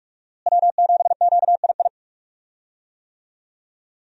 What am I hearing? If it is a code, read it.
Morse code, heard as a keyed tone, about 46 wpm, 700 Hz. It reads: W7QII
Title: USA Call Signs - 5 Characters - Repeated 2x - 45wpm